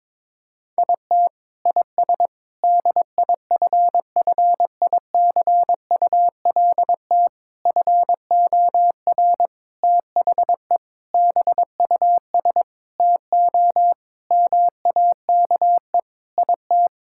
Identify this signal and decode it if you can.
Morse code, heard as a keyed tone, about 21 wpm, 700 Hz. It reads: IT IS DIFFICULT FOR THE BUS TO MAKE IT
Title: Sentences from Top 1000 Words - Review Part 6 - 22wpm